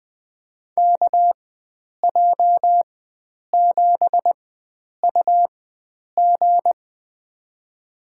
Morse code, heard as a keyed tone, about 20 wpm, spaced as for 9 wpm, 700 Hz. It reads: KJ7UG
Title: USA Call Signs - 5 Characters - 4x Character Spacing - 20wpm